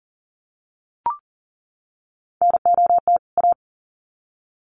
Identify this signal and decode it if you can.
Morse code, heard as a keyed tone, about 40 wpm, 700 Hz. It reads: NOT A
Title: Sets of 2 Words 40wpm